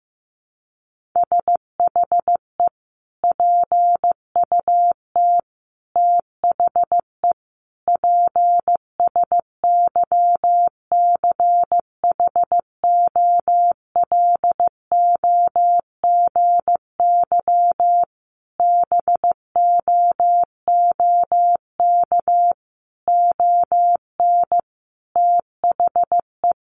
Morse code, heard as a keyed tone, about 15 wpm, 700 Hz. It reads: SHE PUT THE PSYCHOLOGY BOOK ON THE
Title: Sentences from Top 1400 Words 15wpm